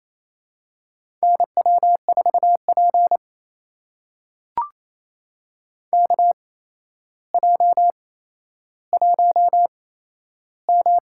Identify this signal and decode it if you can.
Morse code, heard as a keyed tone, about 28 wpm, spaced as for 12 wpm, 700 Hz. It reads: NW4P KJ1M
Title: USA Call Signs - 4 Characters - 8x Character Spacing - 28wpm